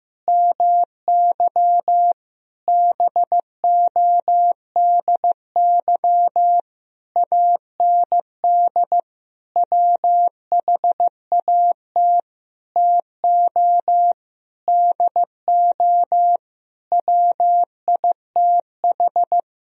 Morse code, heard as a keyed tone, about 15 wpm, 700 Hz. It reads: MY BODY AND WHAT TO DO WITH
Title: Sentences from Top 1000 Words - Review Part 2 - 15wpm